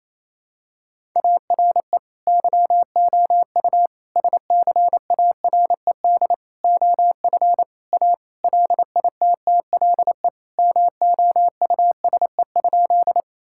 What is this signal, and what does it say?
Morse code, heard as a keyed tone, about 28 wpm, 700 Hz. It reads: ARE YOU SCARED OF A LITTLE MOUSE?
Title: Sentences from Top 1000 Words - Review Part 5 - 28wpm